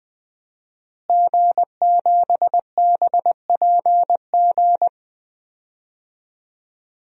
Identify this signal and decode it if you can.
Morse code, heard as a keyed tone, about 20 wpm, 700 Hz. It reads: G7BPG
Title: DXCC-100 Call Signs - 5 Characters - Repeated 2x - 20wpm